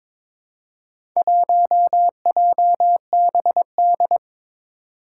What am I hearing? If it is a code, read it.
Morse code, heard as a keyed tone, about 22 wpm, 700 Hz. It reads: 1JBD
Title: Four Challenging Letter-Number Combinations 22wpm